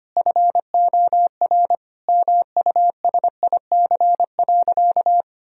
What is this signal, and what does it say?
Morse code, heard as a keyed tone, about 25 wpm, 700 Hz. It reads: FOR MUSIC.
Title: Sentences from Top 1000 Words - Review Part 1 - 25wpm